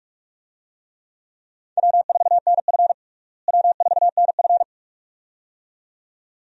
Morse code, heard as a keyed tone, about 41 wpm, 700 Hz. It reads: W4NF W4NF
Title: USA Call Signs - 4 Characters - Repeated 2x - 45wpm